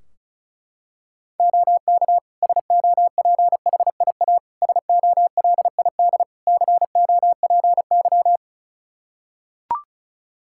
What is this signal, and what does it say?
Morse code, heard as a keyed tone, about 34 wpm, 700 Hz. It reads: OK SOPHIA SOLID COPY
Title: Practice QSO - Exchange 3 - More Info - 35wpm